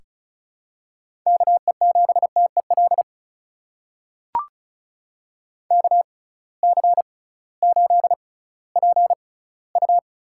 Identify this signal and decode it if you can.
Morse code, heard as a keyed tone, about 36 wpm, spaced as for 19 wpm, 700 Hz. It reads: KE7TEL KC8PU
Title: USA Call Signs - 6x Character Spacing - 35wpm